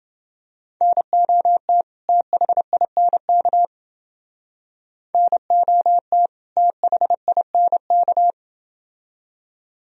Morse code, heard as a keyed tone, about 29 wpm, 700 Hz. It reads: NOT THINK NOT THINK